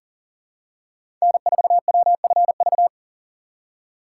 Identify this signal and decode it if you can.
Morse code, heard as a keyed tone, about 40 wpm, 700 Hz. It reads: N4WFV
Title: USA Call Signs - 2x Character Spacing - 40wpm